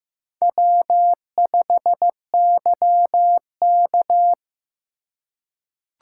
Morse code, heard as a keyed tone, about 15 wpm, 700 Hz. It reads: W5YK
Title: USA Call Signs - 4 Characters - 4x Character Spacing - 15wpm